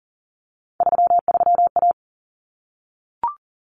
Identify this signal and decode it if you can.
Morse code, heard as a keyed tone, about 40 wpm, 700 Hz. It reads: 33A
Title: Common Cut Numbers - 3-Digit 40wpm